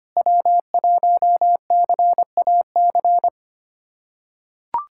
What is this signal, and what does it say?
Morse code, heard as a keyed tone, about 25 wpm, 700 Hz. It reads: W1CAC
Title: USA Callsign DE Callsign - 5 Characters - 25wpm